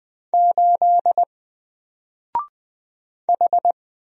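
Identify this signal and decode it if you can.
Morse code, heard as a keyed tone, about 20 wpm, 700 Hz. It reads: 8 H
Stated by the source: Single letter and number Morse drill at 20 wpm, from N4LCD.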